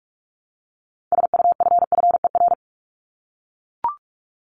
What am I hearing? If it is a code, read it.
Morse code, heard as a keyed tone, about 44 wpm, 700 Hz. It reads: SUFFER
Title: Top 1501-1750 Words 45wpm